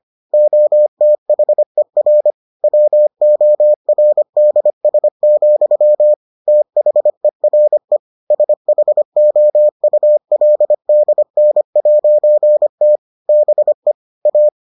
Morse code, heard as a keyed tone, about 25 wpm, 600 Hz. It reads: OTHER WORDS, THERE SHOULDN'T BE A